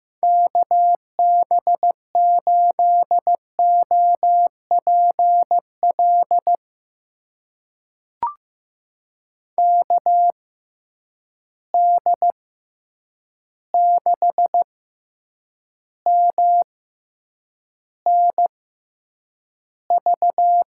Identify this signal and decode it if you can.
Morse code, heard as a keyed tone, about 15 wpm, spaced as for 8 wpm, 700 Hz. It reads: KB8OPL KD6MNV
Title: USA Call Signs - 6 Characters - 6x Character Spacing - 15wpm